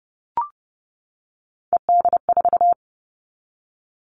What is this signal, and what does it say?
Morse code, heard as a keyed tone, about 30 wpm, 700 Hz. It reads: ED4
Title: All Cut Numbers - 3-Digit 30wpm